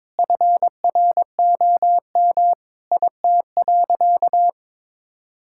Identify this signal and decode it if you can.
Morse code, heard as a keyed tone, about 21 wpm, 700 Hz. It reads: FROM IT.
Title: Sentences from Top 1000 Words - Review Part 5 - 22wpm